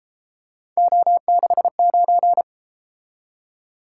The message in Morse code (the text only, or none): O69